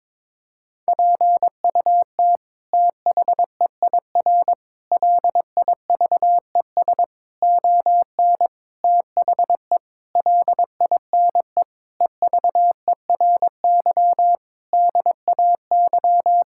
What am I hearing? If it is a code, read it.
Morse code, heard as a keyed tone, about 22 wpm, 700 Hz. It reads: PUT THEIR LIVES ON THE LINE EVERY DAY
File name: practice / 22wpm / Sets of 8 Words - 22wpm